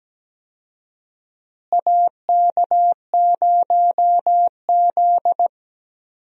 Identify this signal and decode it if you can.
Morse code, heard as a keyed tone, about 17 wpm, 700 Hz. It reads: AK0Z
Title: USA Call Signs - 4 Characters - Repeated 2x - 17wpm